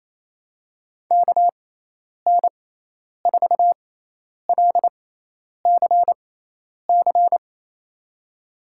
Morse code, heard as a keyed tone, about 28 wpm, spaced as for 10 wpm, 700 Hz. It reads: KN4LCC
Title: USA Call Signs - 6 Characters - 6x Character Spacing - 28wpm